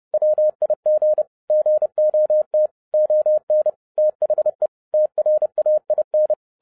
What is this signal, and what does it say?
Morse code, heard as a keyed tone, about 30 wpm, 610 Hz. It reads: WIG GOT ON THE TRAIN